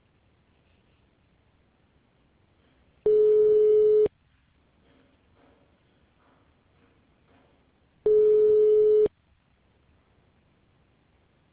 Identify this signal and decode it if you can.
Ringback tone (Europe, 425 Hz).